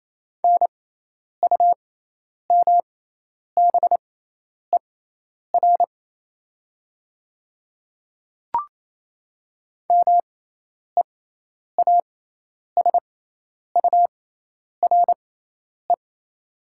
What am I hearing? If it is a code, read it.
Morse code, heard as a keyed tone, about 28 wpm, spaced as for 10 wpm, 700 Hz. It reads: NUMBER MEASURE
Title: 6x Character Spacing - Top 500 Words - 28wpm